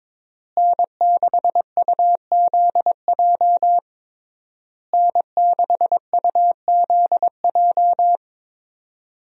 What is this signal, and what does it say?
Morse code, heard as a keyed tone, about 22 wpm, 700 Hz. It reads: N6UZJ N6UZJ
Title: USA Call Signs - 5 Characters - Repeated 3x - 22wpm